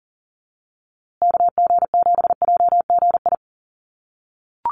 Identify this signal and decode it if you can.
Morse code, heard as a keyed tone, about 40 wpm, 700 Hz. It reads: KG7JGI
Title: USA Call Signs Repeated 3x 40wpm